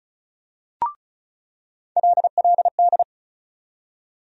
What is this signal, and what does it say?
Morse code, heard as a keyed tone, about 34 wpm, 700 Hz. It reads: LLD
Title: USA Call Sign Postfixes - 3 Letters 35wpm